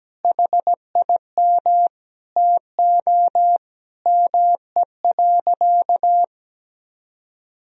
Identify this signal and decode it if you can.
Morse code, heard as a keyed tone, about 17 wpm, 700 Hz. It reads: HIM TO ME.
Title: Sentences from Top 1000 Words - Review Part 3 - 17wpm